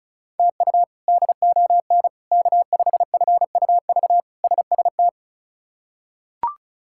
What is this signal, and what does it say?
Morse code, heard as a keyed tone, about 34 wpm, 700 Hz. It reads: TU DON K5FUV SST
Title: K1USN SST QSO Practice - 35wpm